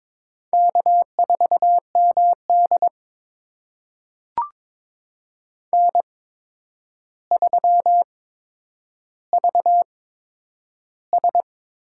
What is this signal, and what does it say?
Morse code, heard as a keyed tone, about 22 wpm, spaced as for 9 wpm, 700 Hz. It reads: K4MD N3VS
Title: USA Call Signs - 4 Characters - 8x Character Spacing - 22wpm